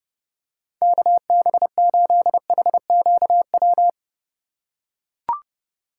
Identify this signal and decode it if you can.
Morse code, heard as a keyed tone, about 31 wpm, 700 Hz. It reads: KB8HQW
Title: USA Call Signs - 6 Characters - 4x Character Spacing - 30wpm